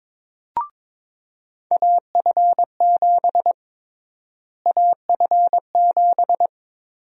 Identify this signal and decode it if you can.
Morse code, heard as a keyed tone, about 22 wpm, 700 Hz. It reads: AF7 AF7
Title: USA Call Sign Prefixes - Repeated 2x - 22wpm